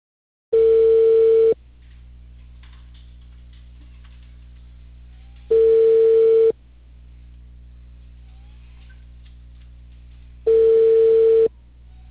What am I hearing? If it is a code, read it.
Ringback tone (China, 450 Hz).